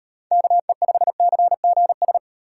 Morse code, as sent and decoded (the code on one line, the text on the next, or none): -.- . ..... -.-. --. ...
KE5CGS